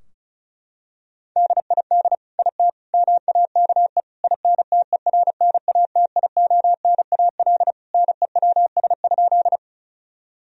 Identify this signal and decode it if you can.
Morse code, heard as a keyed tone, about 36 wpm, 700 Hz. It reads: DID IT MAKE INTERNATIONAL NEWS?
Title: Sentences from Top 1000 Words - Review Part 4 - 35wpm